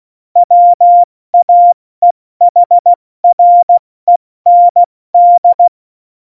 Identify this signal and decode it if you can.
Morse code, heard as a keyed tone, about 16 wpm, spaced as for 14 wpm, 700 Hz. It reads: WAEHREND
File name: Morsekurs / Witze / Witz 10.Dezember 2019, 70 F80